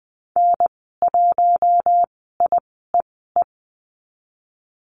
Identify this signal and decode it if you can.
Morse code, heard as a keyed tone, about 20 wpm, spaced as for 14 wpm, 700 Hz. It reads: N1IEE
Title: 069 Lesson 12g - Letter d - Call Signs 20wpm